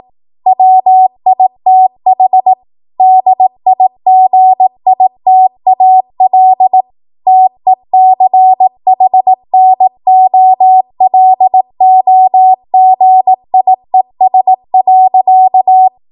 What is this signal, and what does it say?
Morse code, heard as a keyed tone, about 18 wpm, 750 Hz. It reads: WITH DIGITAL TECHNOLOGIES.
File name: Morse / 18WPM